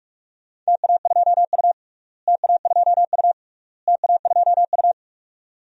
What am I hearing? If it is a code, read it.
Morse code, heard as a keyed tone, about 46 wpm, 700 Hz. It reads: TA2U TA2U TA2U
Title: DXCC-100 Call Signs - 4 Characters - Repeated 3x - 45wpm